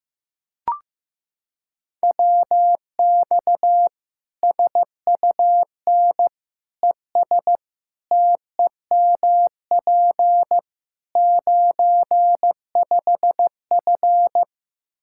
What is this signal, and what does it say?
Morse code, heard as a keyed tone, about 16 wpm, 700 Hz. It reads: WX SUN ES TEMP 95F
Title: Practice QSO - Exchange 2 - Info - 15wpm